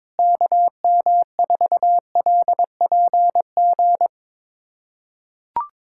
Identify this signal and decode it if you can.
Morse code, heard as a keyed tone, about 22 wpm, 700 Hz. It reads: KM4LPG
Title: USA Callsign DE Callsign - 6 Characters - 22wpm